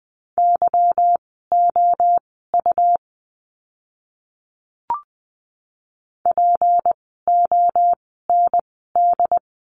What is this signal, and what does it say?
Morse code, heard as a keyed tone, about 20 wpm, spaced as for 15 wpm, 700 Hz. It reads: YOU POND
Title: 153 Lesson 25f - Letter p - Words 20wpm